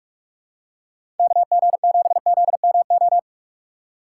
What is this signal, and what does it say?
Morse code, heard as a keyed tone, about 43 wpm, 700 Hz. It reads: KG7ZMO
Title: USA Call Signs - 6 Characters - 2x Character Spacing - 45wpm